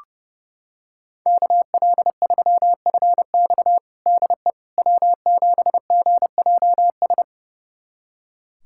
Morse code, heard as a keyed tone, about 32 wpm, 700 Hz. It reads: KL3FX DE W7GJS
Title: USA Callsign DE Callsign - 5 Characters - 30wpm